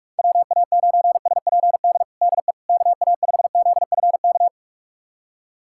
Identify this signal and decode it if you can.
Morse code, heard as a keyed tone, about 46 wpm, 700 Hz. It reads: WA9SPD DE KA5ZFK
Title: USA Callsign DE Callsign - 6 Characters - 45wpm